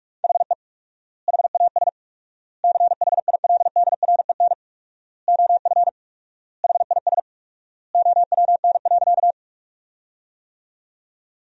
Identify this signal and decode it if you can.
Morse code, heard as a keyed tone, about 43 wpm, 700 Hz. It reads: HE HAS CHILDREN OF HIS OWN.
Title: 4x Word-Spacing - Sentences from Top 300 Words - 45wpm